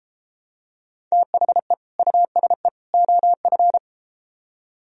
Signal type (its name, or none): Morse code